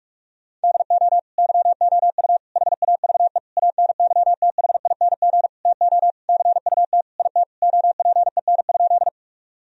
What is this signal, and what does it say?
Morse code, heard as a keyed tone, about 41 wpm, 700 Hz. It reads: DO YOU HAVE ANYTHING TO CUT IT OPEN?
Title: Sentences from Top 1000 Words - Review Part 1 - 45wpm